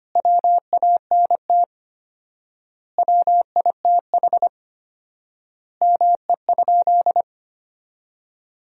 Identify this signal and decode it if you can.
Morse code, heard as a keyed tone, about 25 wpm, 700 Hz. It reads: WANT WITH ME?
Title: 4x Word-Spacing - Sentences from Top 200 Words - 25wpm